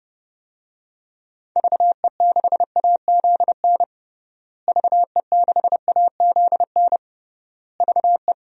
Morse code, heard as a keyed tone, about 32 wpm, 700 Hz. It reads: VE6AZN VE6AZN VE
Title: DXCC-100 Call Signs - 6 Characters - Repeated 3x - 30wpm